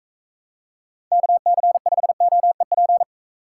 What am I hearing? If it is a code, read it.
Morse code, heard as a keyed tone, about 39 wpm, 700 Hz. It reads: KC5OEP